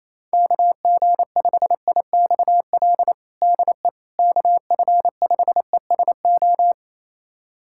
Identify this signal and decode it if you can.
Morse code, heard as a keyed tone, about 28 wpm, 700 Hz. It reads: KG5IXL DE KF5ESO
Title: USA Callsign DE Callsign - 6 Characters - 28wpm